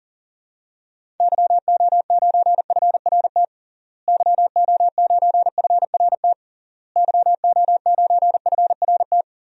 Morse code, heard as a keyed tone, about 40 wpm, 700 Hz. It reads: YO9FRT YO9FRT YO9FRT
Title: DXCC-100 Call Signs - 6 Characters - Repeated 3x - 40wpm